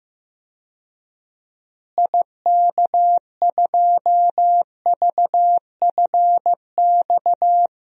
Morse code, heard as a keyed tone, about 15 wpm, 700 Hz. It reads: IK2VFX
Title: DXCC-100 Call Signs - 6 Characters - Repeated 2x - 15wpm